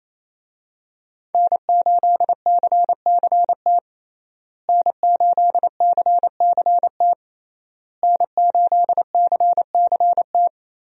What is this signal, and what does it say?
Morse code, heard as a keyed tone, about 28 wpm, 700 Hz. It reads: N8CCT N8CCT N8CCT
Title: USA Call Signs - 5 Characters - Repeated 3x - 28wpm